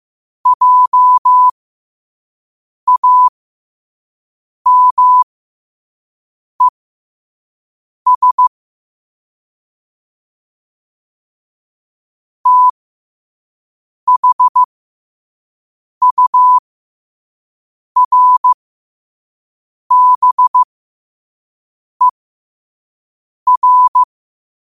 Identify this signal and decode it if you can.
Morse code, heard as a keyed tone, about 15 wpm, spaced as for 5 wpm, 1000 Hz. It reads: JAMES THURBER
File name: 5wpm